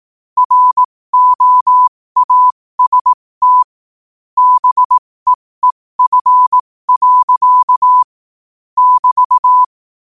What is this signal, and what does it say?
Morse code, heard as a keyed tone, about 18 wpm, spaced as for 15 wpm, 1000 Hz. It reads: ROAST BEEF. <BT>